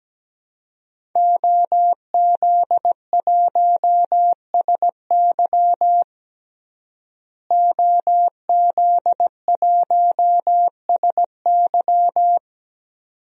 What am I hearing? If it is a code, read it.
Morse code, heard as a keyed tone, about 17 wpm, 700 Hz. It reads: OZ1SY OZ1SY
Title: DXCC-100 Call Signs - 5 Characters - Repeated 2x - 17wpm